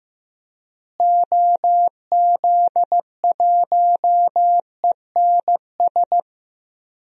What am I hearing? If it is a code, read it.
Morse code, heard as a keyed tone, about 15 wpm, 700 Hz. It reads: OZ1ENS
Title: DXCC-100 Call Signs - 6 Characters - Repeated 2x - 15wpm